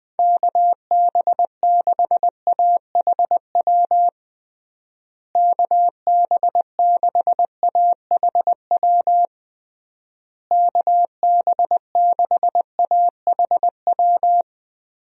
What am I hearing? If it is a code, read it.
Morse code, heard as a keyed tone, about 20 wpm, 700 Hz. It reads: KB6AHW KB6AHW KB6AHW
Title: USA Call Signs - 6 Characters - Repeated 3x - 20wpm